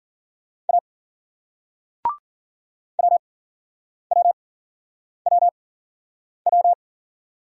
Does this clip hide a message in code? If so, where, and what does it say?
Yes: Morse code from 0.7 to 6.7 s, heard as a keyed tone, about 50 wpm, spaced as for 9 wpm, 700 Hz. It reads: A WWWW